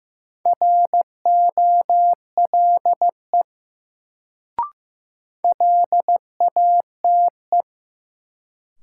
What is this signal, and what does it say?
Morse code, heard as a keyed tone, about 15 wpm, 700 Hz. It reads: ROLE LATE
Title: Sets of 3 Words - New Word Drill - 15wpm